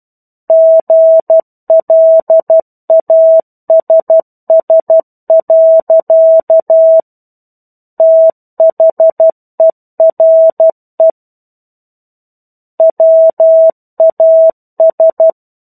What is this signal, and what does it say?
Morse code, heard as a keyed tone, about 12 wpm, 650 Hz. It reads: GLASS. THERE WAS
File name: 01-Chapter-01 12 WPM